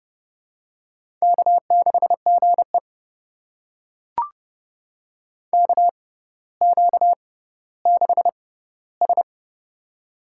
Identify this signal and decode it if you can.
Morse code, heard as a keyed tone, about 30 wpm, spaced as for 15 wpm, 700 Hz. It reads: K6GE KQ6S
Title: USA Call Signs - 6x Character Spacing - 30wpm